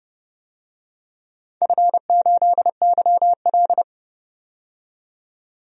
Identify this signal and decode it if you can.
Morse code, heard as a keyed tone, about 30 wpm, 700 Hz. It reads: F8YL
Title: DXCC-100 Call Signs - 4 Characters - 30wpm